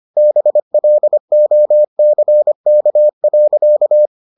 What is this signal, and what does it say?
Morse code, heard as a keyed tone, about 25 wpm, 600 Hz. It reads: BLOCK.